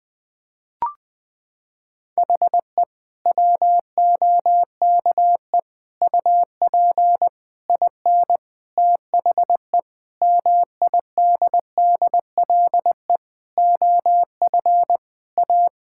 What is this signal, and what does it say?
Morse code, heard as a keyed tone, about 21 wpm, 700 Hz. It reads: HE WOKE UP IN THE MIDDLE OF A
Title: Sentences with 4-Word Phrases - 20wpm